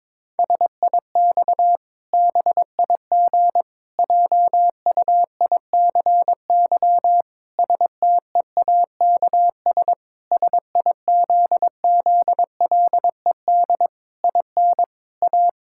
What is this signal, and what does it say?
Morse code, heard as a keyed tone, about 22 wpm, 700 Hz. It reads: SIX BIG JUICY STEAKS SIZZLED IN A
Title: Pangram Challenge v2 - 22wpm